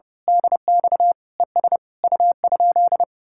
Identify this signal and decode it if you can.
Morse code, heard as a keyed tone, about 28 wpm, 700 Hz. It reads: DX ES U?
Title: Speed-Racing Common QSO Phrases - 30wpm